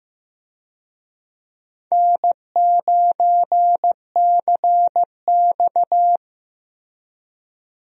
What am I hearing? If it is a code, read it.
Morse code, heard as a keyed tone, about 15 wpm, 700 Hz. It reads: N9CX
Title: USA Call Signs - 4 Characters - Repeated 2x - 15wpm